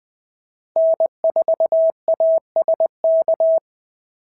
Morse code, heard as a keyed tone, about 20 wpm, 690 Hz. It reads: N4ASK
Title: SOTA QSO Practice 20wpm